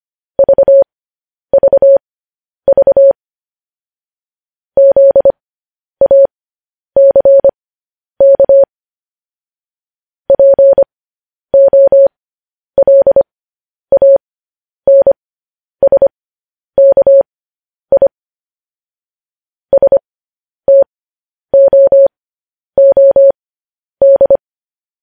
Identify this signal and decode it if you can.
Morse code, heard as a keyed tone, about 25 wpm, spaced as for 10 wpm, 550 Hz. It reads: VVV ZACK POLANSKI STOOD
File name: podcast / News headlines-1x-25x10